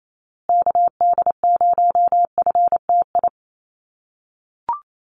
Morse code, heard as a keyed tone, about 28 wpm, 700 Hz. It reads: KD0FTI
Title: USA Call Signs Repeated 3x 28wpm